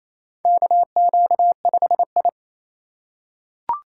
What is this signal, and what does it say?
Morse code, heard as a keyed tone, about 28 wpm, 700 Hz. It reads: KQ5I
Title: USA Call Signs - 3x Character Spacing - 28wpm